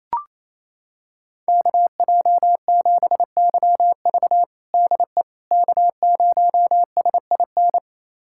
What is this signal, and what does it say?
Morse code, heard as a keyed tone, about 28 wpm, 700 Hz. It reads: KJ7YV DE K0SIN
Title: USA Callsign DE Callsign - 5 Characters - 28wpm